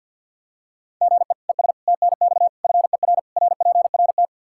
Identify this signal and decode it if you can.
Morse code, heard as a keyed tone, about 51 wpm, 700 Hz. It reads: GE ES TNX FER RPRT
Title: Practice QSO - Exchange 1 - Beginning - 50wpm